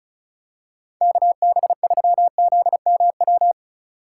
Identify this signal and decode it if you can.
Morse code, heard as a keyed tone, about 34 wpm, 690 Hz. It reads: KB3ZMW
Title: USA Call Signs - 6 Characters - 2x Character Spacing - 35wpm